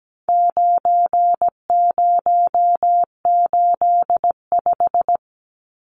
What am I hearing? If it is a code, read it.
Morse code, heard as a keyed tone, about 17 wpm, 700 Hz. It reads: 9085
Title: Numbers - 4-Digit 17wpm